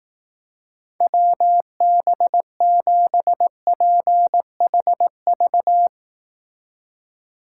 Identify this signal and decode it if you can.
Morse code, heard as a keyed tone, about 18 wpm, 700 Hz. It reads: WB7PHV
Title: Speed Racing - USA 2x3 Call Signs 15wpm